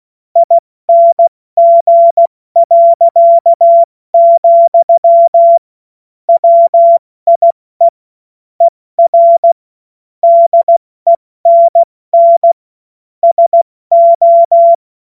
Morse code, heard as a keyed tone, about 16 wpm, 680 Hz. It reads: ING., WIE ER DENN SO
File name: Morsekurs / Witze / Witz 21. Dezember 2021, 70 F80